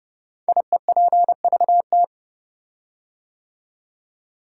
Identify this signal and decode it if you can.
Morse code, heard as a keyed tone, about 30 wpm, 700 Hz. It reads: IEPVT
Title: Code Groups - 5 Letters - 30wpm